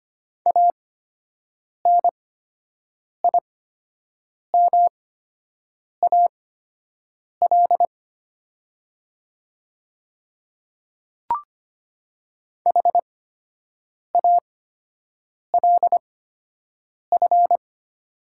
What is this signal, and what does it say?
Morse code, heard as a keyed tone, about 25 wpm, spaced as for 7 wpm, 700 Hz. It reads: ANIMAL HALF
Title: 8x Character Spacing - Top 500 Words - 25wpm